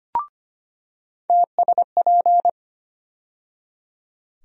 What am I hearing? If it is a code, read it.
Morse code, heard as a keyed tone, about 25 wpm, 700 Hz. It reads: TSP
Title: USA Call Sign Postfixes - 3 Letters 25wpm